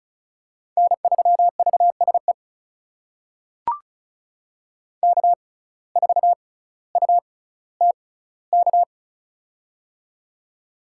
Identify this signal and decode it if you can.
Morse code, heard as a keyed tone, about 36 wpm, spaced as for 18 wpm, 700 Hz. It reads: N3VSE K4UTK
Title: USA Call Signs - 5 Characters - 6x Character Spacing - 35wpm